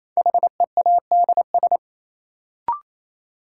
Morse code, heard as a keyed tone, about 28 wpm, 700 Hz. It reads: HEADS
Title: Sentences from Top 1000 Words - Review Part 3 - New Word Drill - 28wpm